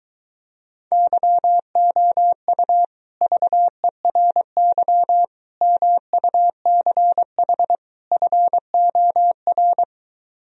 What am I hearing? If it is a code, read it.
Morse code, heard as a keyed tone, about 23 wpm, 700 Hz. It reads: YOU VERY MUCH FOR